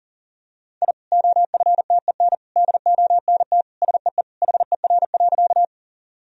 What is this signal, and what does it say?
Morse code, heard as a keyed tone, about 40 wpm, 700 Hz. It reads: I OFTEN DONT SEE HER.
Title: Sentences from Top 1000 Words - Review Part 1 - 40wpm